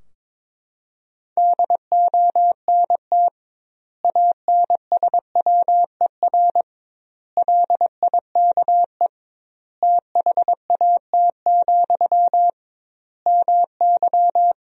Morse code, heard as a keyed tone, about 22 wpm, 700 Hz. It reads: DONT ANSWER LIKE THAT, MY
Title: 2x Word-Spacing - Sentences from Top 400 Words - 22wpm